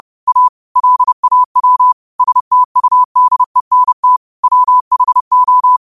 Morse code, heard as a keyed tone, about 30 wpm, 1000 Hz. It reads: A LAW STUDENT WHO